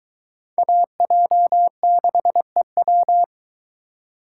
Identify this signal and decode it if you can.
Morse code, heard as a keyed tone, about 22 wpm, 700 Hz. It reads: AJ6EW